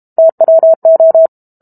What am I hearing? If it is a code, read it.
Morse code, heard as a keyed tone, about 32 wpm, 650 Hz. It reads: TWO
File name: Chapter 6 - 32 WPM